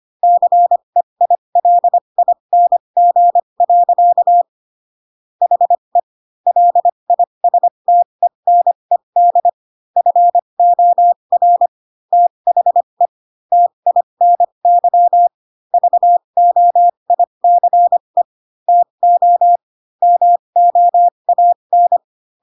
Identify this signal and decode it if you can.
Morse code, heard as a keyed tone, about 25 wpm, spaced as for 22 wpm, 700 Hz. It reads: CEILING. HE LISTENED FOR THE TINY VOICE TO MOAN